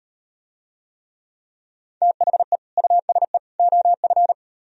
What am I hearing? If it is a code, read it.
Morse code, heard as a keyed tone, about 38 wpm, 700 Hz. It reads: THE USE OF